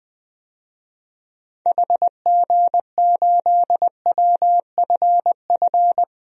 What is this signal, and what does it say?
Morse code, heard as a keyed tone, about 21 wpm, 700 Hz. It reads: HG8WFF
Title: DXCC-100 Call Signs - 6 Characters - 20wpm